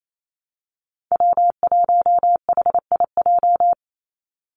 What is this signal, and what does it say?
Morse code, heard as a keyed tone, about 28 wpm, 700 Hz. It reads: W1HIJ
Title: Call Signs Repeated 3x - 28wpm